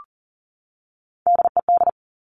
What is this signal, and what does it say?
Morse code, heard as a keyed tone, about 38 wpm, 690 Hz. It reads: DED
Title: Top Three Letter Combinations in Words 40wpm